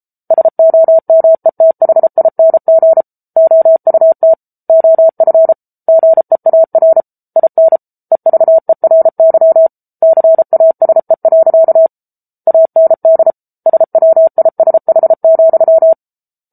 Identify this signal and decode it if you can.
Morse code, heard as a keyed tone, about 34 wpm, 650 Hz. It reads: SOMETHING OUT OF GEAR IN EVERY CASE. AND SWISH,
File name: Book Two / 07-Chapter 7 - 33 WPM